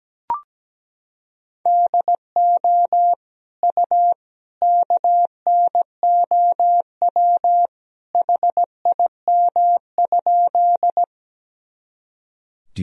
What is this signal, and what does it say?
Morse code, heard as a keyed tone, about 17 wpm, 700 Hz. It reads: DO U KNOW HIM?
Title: Sentences from Top 100 Words and Abbreviations 17wpm